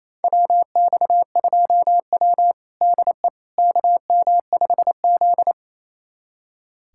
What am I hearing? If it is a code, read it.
Morse code, heard as a keyed tone, about 28 wpm, 700 Hz. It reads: WX2W DE KM5Z
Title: USA Callsign DE Callsign - 4 Characters - 28wpm